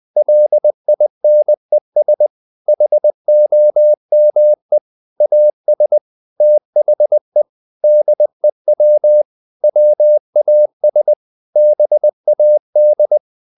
Morse code, heard as a keyed tone, about 20 wpm, 600 Hz. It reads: LINES HOME AS THE DEW WAS BAD
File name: the old man and the sea / the old man and the sea part 1 20wpm